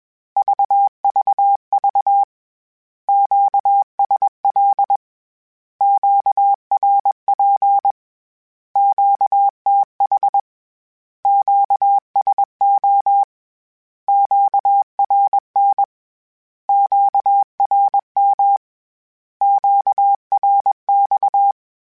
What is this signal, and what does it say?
Morse code, heard as a keyed tone, about 21 wpm, 800 Hz. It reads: VVV QSL QRP QTH QSO QRN QRM QRX
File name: Q Calls / Q Calls-mp3s / Q Calls Once @20